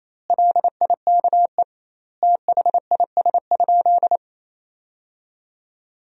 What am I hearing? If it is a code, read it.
Morse code, heard as a keyed tone, about 28 wpm, 700 Hz. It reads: LIKE THIS?
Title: 2x Word-Spacing - Sentences from Top 200 Words - 28wpm